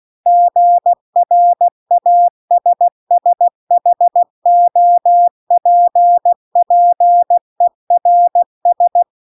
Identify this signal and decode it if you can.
Morse code, heard as a keyed tone, about 16 wpm, 700 Hz. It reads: GRASSHOPPERS